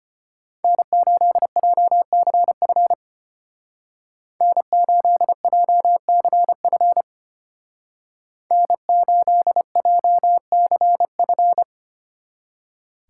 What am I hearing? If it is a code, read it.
Morse code, heard as a keyed tone, about 31 wpm, 700 Hz. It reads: N8JCF N8JCF N8JCF
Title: Speed Racing - USA 1x3 Call Signs 25wpm